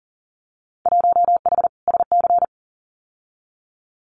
Morse code, heard as a keyed tone, about 40 wpm, 700 Hz. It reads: 1H SC
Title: Winter Field Day Exchanges 40wpm